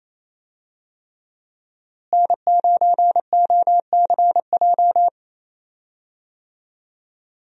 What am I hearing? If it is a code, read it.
Morse code, heard as a keyed tone, about 28 wpm, 700 Hz. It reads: N9OCJ
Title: USA Call Signs - 5 Characters - Repeated 3x - 28wpm